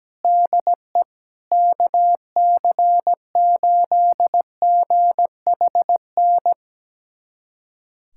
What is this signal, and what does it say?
Morse code, heard as a keyed tone, about 17 wpm, 700 Hz. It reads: DE KC8GHN
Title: USA Callsign DE Callsign - 6 Characters - 17wpm